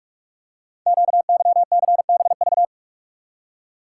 Morse code, heard as a keyed tone, about 44 wpm, 690 Hz. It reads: QYCBV
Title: Code Groups - 5 Letters - 45wpm